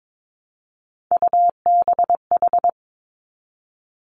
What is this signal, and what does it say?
Morse code, heard as a keyed tone, about 22 wpm, 700 Hz. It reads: UBH
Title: Three Challenging Letter-Number Combinations 22wpm